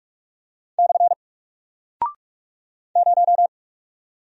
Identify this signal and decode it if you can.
Morse code, heard as a keyed tone, about 44 wpm, 700 Hz. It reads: / 0
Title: Single Character (Letter:Number:Punctuation:Prosign) Alt 45wpm